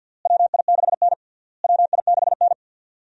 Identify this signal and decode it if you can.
Morse code, heard as a keyed tone, about 53 wpm, 700 Hz. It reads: WI6N WI6N
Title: USA Call Signs - 4 Characters - Repeated 3x - 50wpm